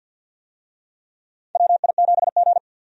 Morse code, heard as a keyed tone, about 46 wpm, 700 Hz. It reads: WI7G
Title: USA Call Signs - 4 Characters - 50wpm